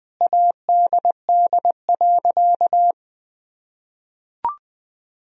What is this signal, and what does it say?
Morse code, heard as a keyed tone, about 21 wpm, 700 Hz. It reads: ADD.
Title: Sentences from Top 1000 Words - Review Part 1 - 20wpm